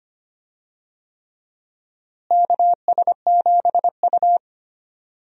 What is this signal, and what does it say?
Morse code, heard as a keyed tone, about 25 wpm, 700 Hz. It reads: KS7U
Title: USA Call Signs - 4 Characters - Repeated 3x - 25wpm